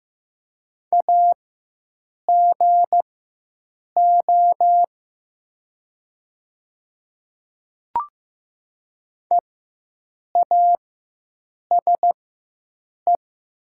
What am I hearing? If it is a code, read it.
Morse code, heard as a keyed tone, about 15 wpm, spaced as for 7 wpm, 700 Hz. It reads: AGO EASE
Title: 4x Character Spacing - Top 500 Words - 15wpm